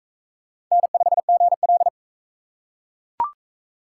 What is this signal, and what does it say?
Morse code, heard as a keyed tone, about 38 wpm, 700 Hz. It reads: N5GL